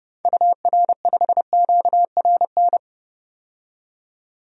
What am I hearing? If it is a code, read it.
Morse code, heard as a keyed tone, about 30 wpm, 700 Hz. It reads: UR5QRN
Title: DXCC-100 Call Signs - 6 Characters - 30wpm